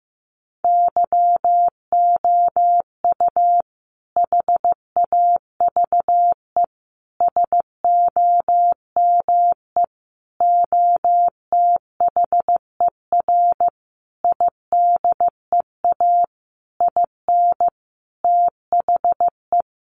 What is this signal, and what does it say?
Morse code, heard as a keyed tone, about 15 wpm, 700 Hz. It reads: YOU HAVE SOME OTHER IDEA IN THE
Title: Sentences from Top 1300 Words 15wpm